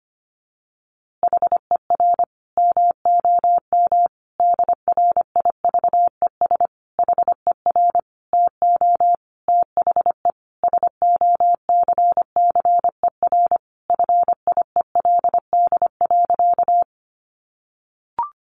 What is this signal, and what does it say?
Morse code, heard as a keyed tone, about 25 wpm, 700 Hz. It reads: HER MOM DRIVES HER TO THE SOCCER FIELD.
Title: Easy Phrases 25wpm